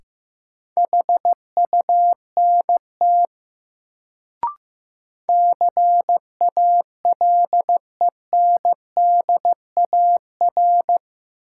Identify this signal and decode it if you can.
Morse code, heard as a keyed tone, about 15 wpm, 700 Hz. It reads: HUNT CALENDAR
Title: Sentences from Top 1600 Words - New Word Drill Alt - 15wpm